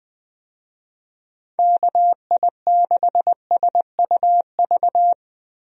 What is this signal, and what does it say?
Morse code, heard as a keyed tone, about 20 wpm, 700 Hz. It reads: KI6SUV
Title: USA Call Signs - 6 Characters - Repeated 2x - 20wpm